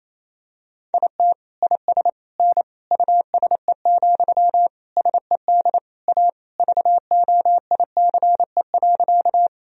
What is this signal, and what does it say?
Morse code, heard as a keyed tone, about 28 wpm, 700 Hz. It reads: IT IS N USE, SED A VOICE.
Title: Sentences from Top 500 Words and Abbreviations 28wpm